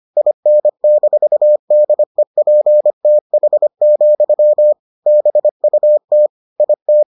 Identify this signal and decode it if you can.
Morse code, heard as a keyed tone, about 25 wpm, 600 Hz. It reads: IN-DEPTH, BUT IT